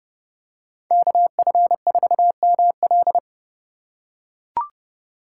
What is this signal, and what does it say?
Morse code, heard as a keyed tone, about 30 wpm, 700 Hz. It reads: KF4ML